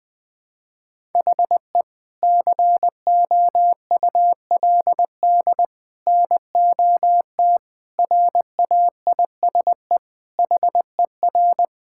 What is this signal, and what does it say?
Morse code, heard as a keyed tone, about 20 wpm, 700 Hz. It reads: HE COULD NOT RAISE HER
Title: Sentences from Top 1600 Words 20wpm